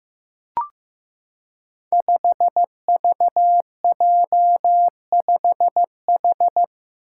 Morse code, heard as a keyed tone, about 15 wpm, 700 Hz. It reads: 5VJ5H
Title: Five Challenging Letter-Number Combinations 15wpm